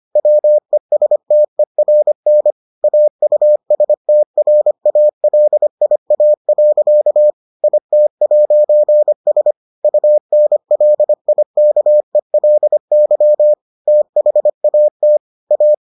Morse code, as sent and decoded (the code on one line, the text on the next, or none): .-- . ... - . .-. -. / .- ..- ... - .-. .- .-.. .. .- .-.-.- / .. - .----. ... / ..- -. .-.. .. -.- . .-.. -.-- / - .... .- - / .-
WESTERN AUSTRALIA. IT'S UNLIKELY THAT A